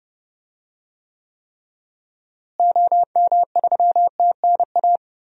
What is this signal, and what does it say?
Morse code, heard as a keyed tone, about 30 wpm, 700 Hz. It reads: OM3TNA
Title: DXCC-100 Call Signs - 6 Characters - Repeated 3x - 30wpm